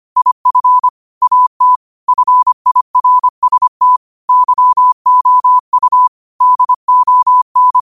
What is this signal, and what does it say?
Morse code, heard as a keyed tone, about 25 wpm, 1000 Hz. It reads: IF AT FIRST YOU DON
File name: qotdfiles / 25wpm